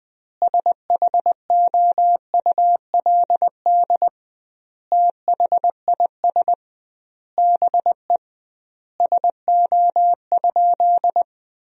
Morse code, heard as a keyed tone, about 20 wpm, 700 Hz. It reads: SHOULD THIS BE SO?
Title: 2x Word-Spacing - Sentences from Top 300 Words - 20wpm